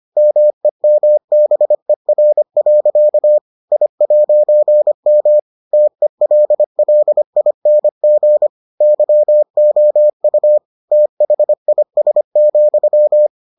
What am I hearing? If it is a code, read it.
Morse code, heard as a keyed tone, about 25 wpm, 600 Hz. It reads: MEMBER. I'M TELLING YOU THIS,